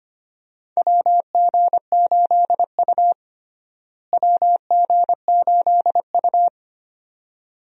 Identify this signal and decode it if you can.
Morse code, heard as a keyed tone, about 25 wpm, 700 Hz. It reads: WG8U WG8U
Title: USA Call Signs - 4 Characters - Repeated 2x - 25wpm